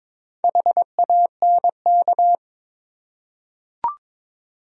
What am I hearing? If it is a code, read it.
Morse code, heard as a keyed tone, about 22 wpm, 700 Hz. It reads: HANK
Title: USA Common Names - Max 5 Letters - 22wpm